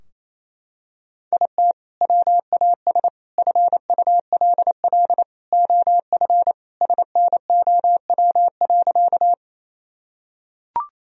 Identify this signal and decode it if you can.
Morse code, heard as a keyed tone, about 28 wpm, 700 Hz. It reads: IT WAS FULL OF SNOW.
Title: Sentences from Top 1000 Words - Review Part 3 - 28wpm